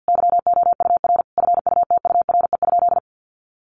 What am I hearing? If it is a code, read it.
Morse code, heard as a keyed tone, about 49 wpm, 700 Hz. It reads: YOUR FUTURE?
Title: Sentences from Top 700 Words 50wpm